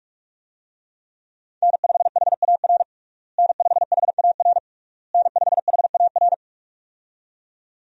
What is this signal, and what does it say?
Morse code, heard as a keyed tone, about 46 wpm, 700 Hz. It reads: N5HAR N5HAR N5HAR
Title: USA Call Signs - 5 Characters - Repeated 3x - 45wpm